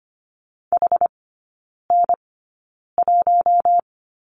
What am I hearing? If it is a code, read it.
Morse code, heard as a keyed tone, about 25 wpm, spaced as for 9 wpm, 700 Hz. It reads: HN1
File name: Lesson 14e - Number 2 - Three Characters 25wpm